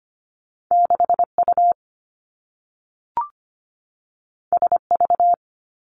Morse code, heard as a keyed tone, about 25 wpm, 700 Hz. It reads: 6U SV
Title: Two Challenging Letter-Number Combinations 25wpm